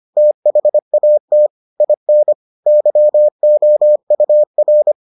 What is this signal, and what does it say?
Morse code, heard as a keyed tone, about 25 wpm, 600 Hz. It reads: THAT IN YOUR